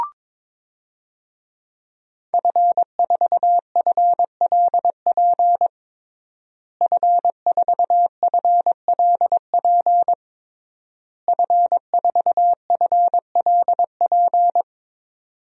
Morse code, heard as a keyed tone, about 21 wpm, 700 Hz. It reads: F4FLP F4FLP F4FLP
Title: DXCC-100 Call Signs - 5 Characters - Repeated 3x - 22wpm